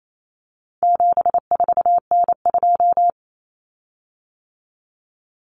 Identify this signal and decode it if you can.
Morse code, heard as a keyed tone, about 28 wpm, 700 Hz. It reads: 74N2
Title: Common Cut Numbers - 4-Digit 28wpm